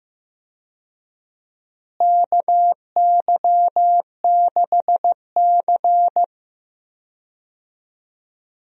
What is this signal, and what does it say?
Morse code, heard as a keyed tone, about 15 wpm, 700 Hz. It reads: KY6C
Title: USA Call Signs - 4 Characters - Repeated 2x - 15wpm